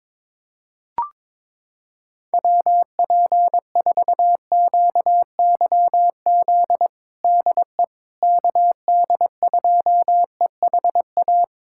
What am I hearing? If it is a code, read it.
Morse code, heard as a keyed tone, about 22 wpm, 700 Hz. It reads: WP4QYZ DE KD2EHA
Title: USA Callsign DE Callsign - 6 Characters - 22wpm